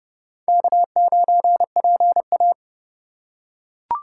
Morse code, heard as a keyed tone, about 29 wpm, 700 Hz. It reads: K9PA